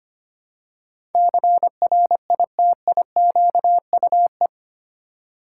Morse code, heard as a keyed tone, about 25 wpm, 700 Hz. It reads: CRITIQUE
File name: Eight Letter Words - 25wpm